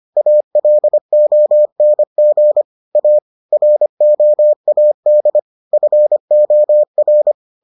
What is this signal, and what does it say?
Morse code, heard as a keyed tone, about 25 wpm, 600 Hz. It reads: ALONG A ROAD FOR